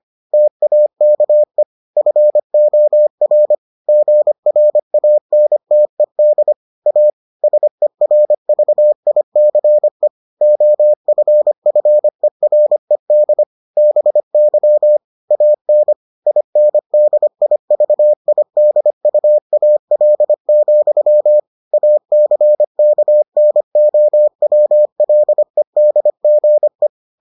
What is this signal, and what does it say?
Morse code, heard as a keyed tone, about 25 wpm, 600 Hz. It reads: TAKE FOR GRANTED A SERVICE OFFERED BY AN INDIVIDUAL, ACKNOWLEDGE